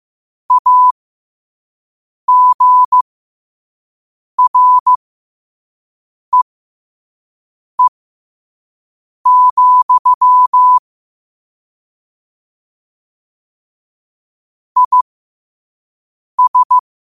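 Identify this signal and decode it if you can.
Morse code, heard as a keyed tone, about 15 wpm, spaced as for 5 wpm, 1000 Hz. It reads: AGREE, IS
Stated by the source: Morse code sent at 5 wpm.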